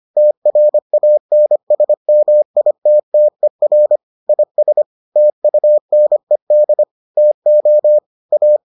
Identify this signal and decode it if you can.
Morse code, heard as a keyed tone, about 25 wpm, 600 Hz. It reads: TRANSMITTER IS TUNED TO A